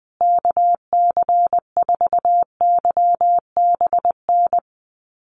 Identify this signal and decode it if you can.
Morse code, heard as a keyed tone, about 20 wpm, 700 Hz. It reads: KC4YBN
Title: USA Call Signs Repeated 3x 20wpm